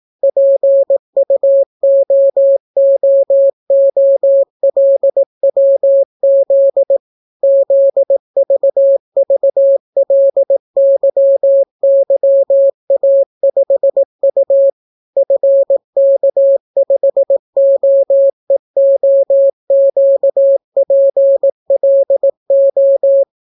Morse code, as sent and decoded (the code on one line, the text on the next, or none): .--. ..- --- --- --- .-.. .-- --.. / --.. ...- ...- .-.. -.-- -.-- .- ..... ..- / ..-. -.- ..... --- . --- --.- .--. .-.. ---
PUOOOLWZ ZVVLYYA5U FK5OEOQPLO